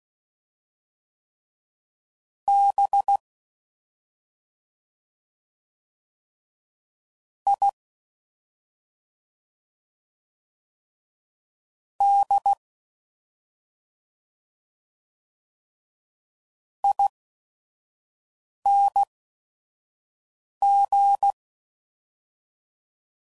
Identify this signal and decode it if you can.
Morse code, heard as a keyed tone, about 16 wpm, spaced as for 3 wpm, 800 Hz. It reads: BIDING